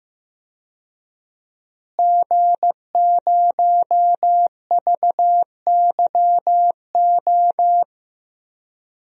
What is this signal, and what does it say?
Morse code, heard as a keyed tone, about 15 wpm, 700 Hz. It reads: G0VYO
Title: DXCC-100 Call Signs - 5 Characters - Repeated 3x - 15wpm